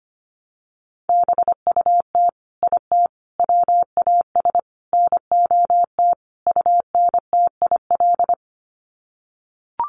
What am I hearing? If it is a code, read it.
Morse code, heard as a keyed tone, about 25 wpm, 700 Hz. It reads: BUT IT WAS NOT UNTIL
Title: Sets of 5 Words Encore2 25wpm